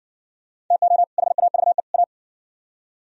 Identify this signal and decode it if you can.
Morse code, heard as a keyed tone, about 58 wpm, 700 Hz. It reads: TO HAVE A